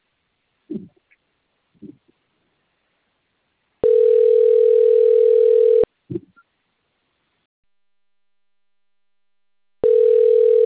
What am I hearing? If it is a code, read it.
Ringback tone (North America), 440 + 480 Hz.